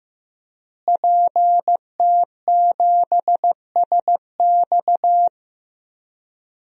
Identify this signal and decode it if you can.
Morse code, heard as a keyed tone, about 16 wpm, 700 Hz. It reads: PT7SX
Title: DXCC-100 Call Signs - 5 Characters - Repeated 2x - 15wpm